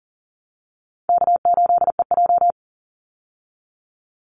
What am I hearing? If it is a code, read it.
Morse code, heard as a keyed tone, about 40 wpm, 700 Hz. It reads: K8EJ
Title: Call Signs Repeated 2x - 40wpm